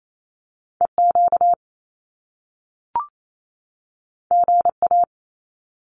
Morse code, heard as a keyed tone, about 28 wpm, 700 Hz. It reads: EQ GA
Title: Top Two Letter Combinations in Words 28wpm